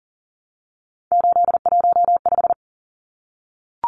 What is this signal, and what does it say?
Morse code, heard as a keyed tone, about 40 wpm, 700 Hz. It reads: 815
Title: Numbers - 3-Digit 40wpm